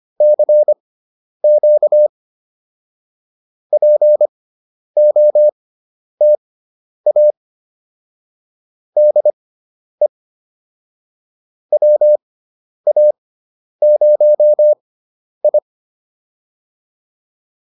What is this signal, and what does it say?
Morse code, heard as a keyed tone, about 25 wpm, spaced as for 10 wpm, 600 Hz. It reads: CQ POTA DE WA0I
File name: POTA107 10